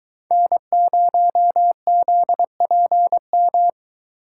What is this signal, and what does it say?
Morse code, heard as a keyed tone, about 22 wpm, 700 Hz. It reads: N0ZPM